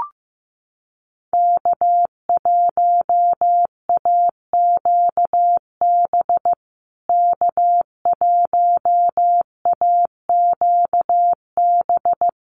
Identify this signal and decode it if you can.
Morse code, heard as a keyed tone, about 15 wpm, 700 Hz. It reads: K1AQB K1AQB
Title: USA Call Signs Repeated 3x 15wpm